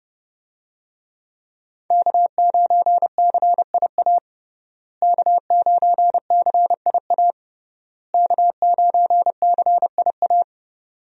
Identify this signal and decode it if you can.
Morse code, heard as a keyed tone, about 28 wpm, 700 Hz. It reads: K9CIA K9CIA K9CIA
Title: USA Call Signs - 5 Characters - Repeated 3x - 30wpm